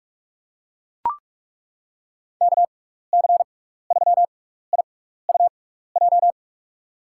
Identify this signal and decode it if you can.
Morse code, heard as a keyed tone, about 42 wpm, spaced as for 15 wpm, 700 Hz. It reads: KC3IUJ
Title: USA Call Signs - 6 Characters - 6x Character Spacing - 45wpm